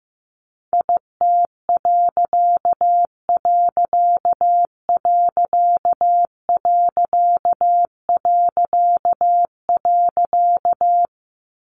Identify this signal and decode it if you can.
Morse code, heard as a keyed tone, about 15 wpm, 700 Hz. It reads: IT......
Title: A Sound Of Thunder 15wpm